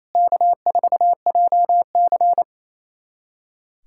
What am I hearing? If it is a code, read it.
Morse code, heard as a keyed tone, about 28 wpm, 700 Hz. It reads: K4JC
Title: USA Callsign DE Callsign - 4 Characters - 28wpm